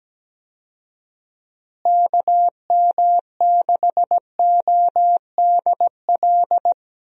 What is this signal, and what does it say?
Morse code, heard as a keyed tone, about 17 wpm, 700 Hz. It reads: KM6ODL
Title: USA Call Signs - 6 Characters - Repeated 2x - 17wpm